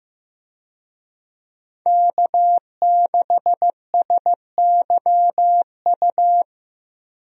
Morse code, heard as a keyed tone, about 16 wpm, 700 Hz. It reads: K6SYU
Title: USA Call Signs - 5 Characters - Repeated 2x - 15wpm